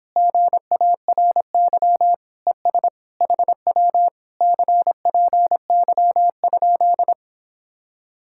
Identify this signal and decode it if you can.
Morse code, heard as a keyed tone, about 26 wpm, 700 Hz. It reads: GARY ES HW CPY?